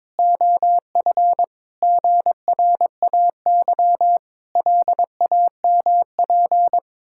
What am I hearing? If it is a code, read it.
Morse code, heard as a keyed tone, about 22 wpm, 700 Hz. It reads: OF GRAY LAMP
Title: Pangram Challenge v2 - 22wpm